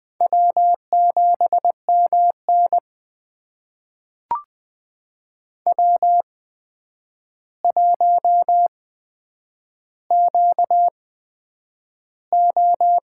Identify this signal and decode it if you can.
Morse code, heard as a keyed tone, about 20 wpm, spaced as for 9 wpm, 700 Hz. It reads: W7MN W1QO
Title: USA Call Signs - 4 Characters - 8x Character Spacing - 20wpm